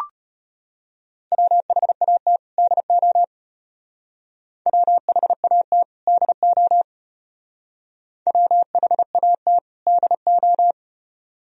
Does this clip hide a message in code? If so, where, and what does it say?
Yes: Morse code from 1.3 to 10.7 s, heard as a keyed tone, about 33 wpm, 700 Hz. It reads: WHAT DO WHAT DO WHAT DO